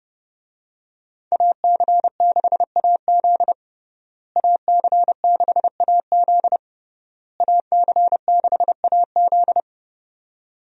Morse code, heard as a keyed tone, about 30 wpm, 700 Hz. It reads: AC6AZ AC6AZ AC6AZ
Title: USA Call Signs - 5 Characters - Repeated 3x - 30wpm